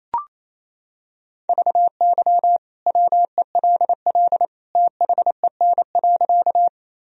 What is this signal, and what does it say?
Morse code, heard as a keyed tone, about 28 wpm, 700 Hz. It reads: VY WELL THEN.
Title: Sentences from Top 200 Words and Abbreviations 28wpm